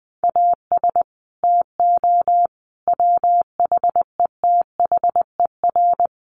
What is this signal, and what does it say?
Morse code, heard as a keyed tone, about 20 wpm, 700 Hz. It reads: AS TO WHETHER
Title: Sets of 5 Words Encore 20wpm